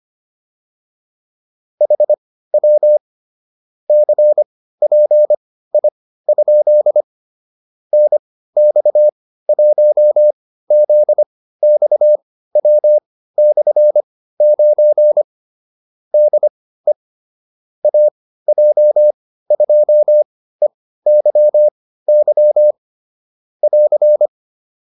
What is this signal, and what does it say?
Morse code, heard as a keyed tone, about 25 wpm, spaced as for 15 wpm, 600 Hz. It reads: HW CPI? NX1ZXW/9 DE AJ2EYY <AR>
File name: QSO307 15